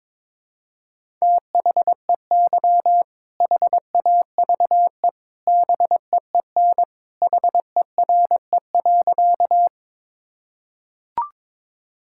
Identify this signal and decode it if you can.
Morse code, heard as a keyed tone, about 22 wpm, 700 Hz. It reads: THEY HAVE BEEN HERE.
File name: practice / 22wpm / Sentences with 2-Word Phrases - 22wpm